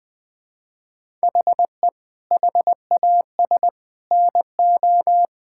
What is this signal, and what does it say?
Morse code, heard as a keyed tone, about 20 wpm, 700 Hz. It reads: HE HAS NO
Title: Sentences from Top 1000 Words - Review Part 5 - 20wpm